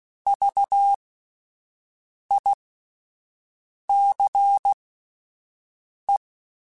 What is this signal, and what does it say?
Morse code, heard as a keyed tone, about 16 wpm, spaced as for 5 wpm, 800 Hz. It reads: VICE